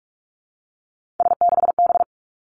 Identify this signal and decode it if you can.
Morse code, heard as a keyed tone, about 46 wpm, 700 Hz. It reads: S6B
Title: Three Challenging Letter-Number Combinations 45wpm